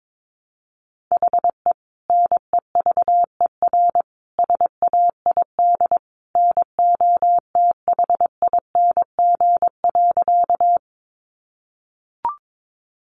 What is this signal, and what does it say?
Morse code, heard as a keyed tone, about 22 wpm, 700 Hz. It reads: HE NEVER SAID NOTHING.
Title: Sentences from Top 500 Words 22wpm